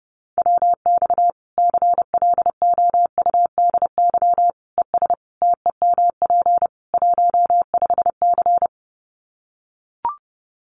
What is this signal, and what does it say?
Morse code, heard as a keyed tone, about 30 wpm, 700 Hz. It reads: WX CLOUDY ES TEMP 15C
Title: Common QSO Phrases 30wpm